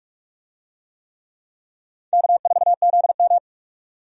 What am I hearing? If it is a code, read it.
Morse code, heard as a keyed tone, about 45 wpm, 700 Hz. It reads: K4ZM
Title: USA Call Signs - 4 Characters - 45wpm